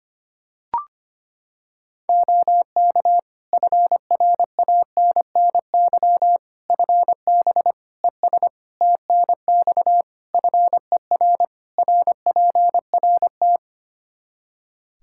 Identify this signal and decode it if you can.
Morse code, heard as a keyed tone, about 25 wpm, 700 Hz. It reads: OK FRANNY FB ES TNX FER RPRT
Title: Practice QSO - Exchange 2 - Info - 25wpm